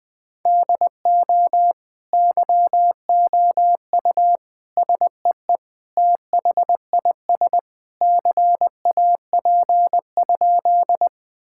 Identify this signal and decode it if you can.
Morse code, heard as a keyed tone, about 20 wpm, 700 Hz. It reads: DO YOU SEE THIS CAP?
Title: Sentences from Top 1600 Words 20wpm